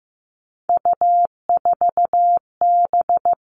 Morse code, heard as a keyed tone, about 15 wpm, 700 Hz. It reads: U4B
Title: Three Challenging Letter-Number Combinations 15wpm